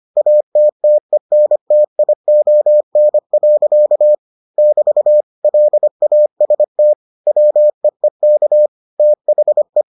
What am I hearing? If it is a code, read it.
Morse code, heard as a keyed tone, about 25 wpm, 600 Hz. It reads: ATTENTION. <BT> LAST WEEK THE